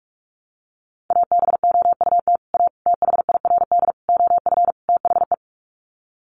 Morse code, heard as a keyed tone, about 44 wpm, 700 Hz. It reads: ABOUT A THIRD OF THE
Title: Sets of 5 Words Encore2 45wpm